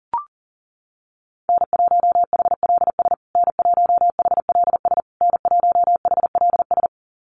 Morse code, heard as a keyed tone, about 37 wpm, 700 Hz. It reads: N1HLS N1HLS N1HLS
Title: USA Call Signs Repeated 3x 40wpm